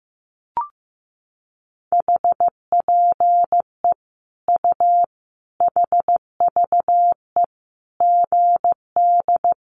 Morse code, heard as a keyed tone, about 15 wpm, 700 Hz. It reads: HPE U HVE GD
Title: Common QSO Phrases 15wpm